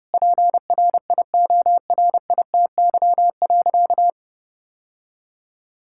Morse code, heard as a keyed tone, about 32 wpm, 700 Hz. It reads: PRIORITY.
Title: Sentences from Top 1600 Words 30wpm